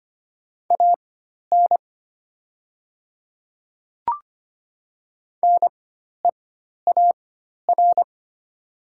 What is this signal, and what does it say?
Morse code, heard as a keyed tone, about 25 wpm, spaced as for 12 wpm, 700 Hz. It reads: AN NEAR
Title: 4x Character Spacing - Top 200 Words - 25wpm